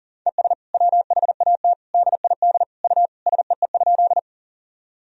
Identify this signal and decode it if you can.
Morse code, heard as a keyed tone, about 40 wpm, 700 Hz. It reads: ES WHAT DID U SEE?
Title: Sentences from Top 100 Words and Abbreviations 40wpm